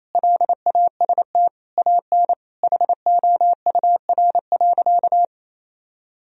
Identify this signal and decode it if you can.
Morse code, heard as a keyed tone, about 28 wpm, 700 Hz. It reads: LAST AN HOUR.
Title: Sentences from Top 1000 Words - Review Part 1 - 28wpm